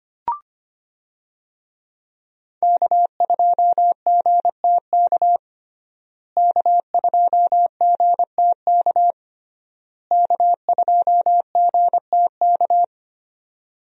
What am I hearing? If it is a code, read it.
Morse code, heard as a keyed tone, about 25 wpm, 700 Hz. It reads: K2GTK K2GTK K2GTK
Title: USA Call Signs - 5 Characters - Repeated 3x - 25wpm